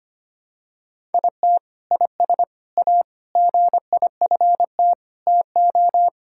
Morse code, heard as a keyed tone, about 25 wpm, 700 Hz. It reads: IT IS A GIFT TO
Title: Sentences from Top 1000 Words - Review Part 6 - 25wpm